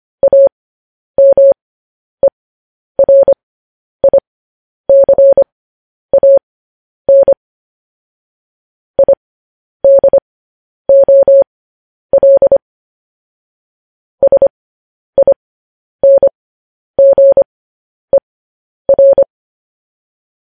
Morse code, heard as a keyed tone, about 25 wpm, spaced as for 10 wpm, 550 Hz. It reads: AMERICAN IDOL SINGER
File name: podcast / News headlines-1x-25x10